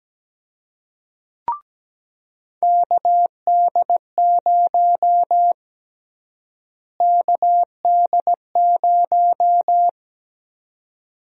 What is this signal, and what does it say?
Morse code, heard as a keyed tone, about 17 wpm, 700 Hz. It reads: KD0 KD0
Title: USA Call Sign Prefixes - Repeated 2x - 17wpm